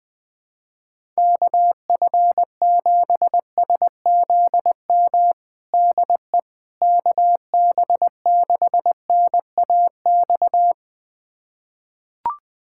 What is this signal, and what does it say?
Morse code, heard as a keyed tone, about 20 wpm, 700 Hz. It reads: KF7SZM DE KB6NAX
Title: USA Callsign DE Callsign - 6 Characters - 20wpm